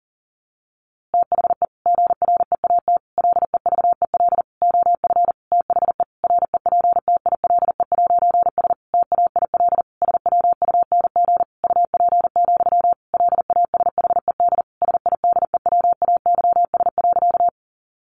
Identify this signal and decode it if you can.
Morse code, heard as a keyed tone, about 40 wpm, 700 Hz. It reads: THE GREAT LEVEL OF THE REPTILE'S TAIL SWUNG UP, LASHED SIDEWAYS.
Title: A Sound Of Thunder 40wpm